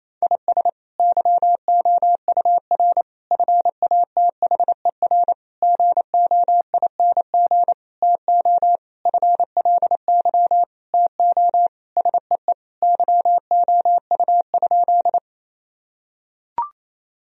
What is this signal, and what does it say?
Morse code, heard as a keyed tone, about 28 wpm, 700 Hz. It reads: IS YOUR FATHER GOING TO FLY TO SEE YOU?
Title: Sentences from Top 1000 Words - Review Part 2 - 28wpm